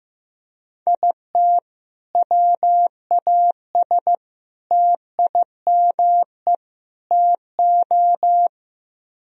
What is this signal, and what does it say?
Morse code, heard as a keyed tone, about 16 wpm, 700 Hz. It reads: IT WAS TIME TO
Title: Speed-Racing - Sets of 4 Words 15wpm